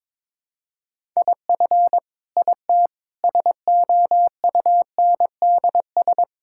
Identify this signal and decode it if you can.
Morse code, heard as a keyed tone, about 22 wpm, 700 Hz. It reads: IF IT SOUNDS
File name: Sentences with 9-Word Phrases - 22wpm